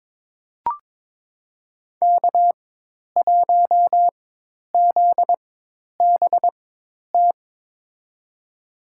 Morse code, heard as a keyed tone, about 22 wpm, spaced as for 10 wpm, 700 Hz. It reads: K1ZBT
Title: USA Call Signs - 5 Characters - 4x Character Spacing - 22wpm